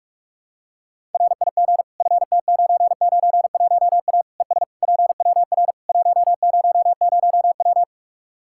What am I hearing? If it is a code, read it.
Morse code, heard as a keyed tone, about 45 wpm, 700 Hz. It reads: RIG FT991A ES PWR 100W
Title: K1USN SST QSO Practice 45wpm